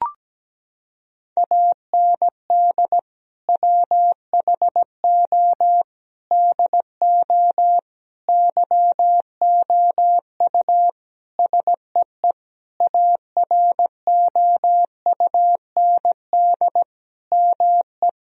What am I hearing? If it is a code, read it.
Morse code, heard as a keyed tone, about 17 wpm, 700 Hz. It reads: AND WHO DO YOU SEE AROUND ME
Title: Sentences from Top 1000 Words - Review Part 3 - 17wpm